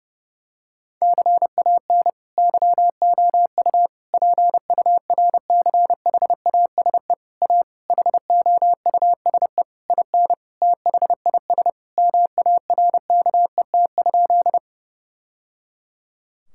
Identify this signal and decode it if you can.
Morse code, heard as a keyed tone, about 32 wpm, 700 Hz. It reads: CAN YOU PURCHASE A HOUSE IN THIS MARKET?
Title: Sentences from Top 1000 Words - Review Part 5 - 30wpm